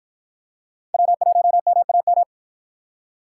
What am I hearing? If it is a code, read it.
Morse code, heard as a keyed tone, about 52 wpm, 700 Hz. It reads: W1MAM